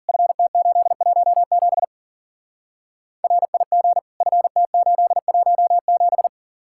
Morse code, heard as a keyed tone, about 49 wpm, 700 Hz. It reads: FT817 RIG FT817
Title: Speed-Racing Common QSO Phrases - 40wpm